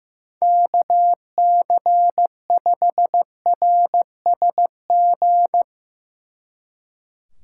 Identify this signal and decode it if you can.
Morse code, heard as a keyed tone, about 15 wpm, 700 Hz. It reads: KC5RSG
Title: Speed Racing - USA 2x3 Call Signs 15wpm